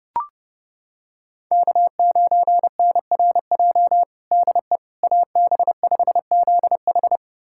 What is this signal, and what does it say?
Morse code, heard as a keyed tone, about 30 wpm, 700 Hz. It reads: K9NRJ DE AB5ZH
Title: USA Callsign DE Callsign - 5 Characters - 30wpm